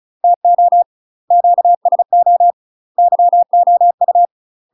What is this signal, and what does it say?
Morse code, heard as a keyed tone, about 36 wpm, 700 Hz. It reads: TO QSO YOU
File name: UK MASTER QSOs TRAINING / 35WPM / UK MASTER QSOs 0008